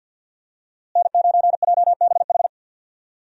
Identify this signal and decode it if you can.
Morse code, heard as a keyed tone, about 52 wpm, 700 Hz. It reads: N9JBH
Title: USA Call Signs - 5 Characters - Repeated 2x - 50wpm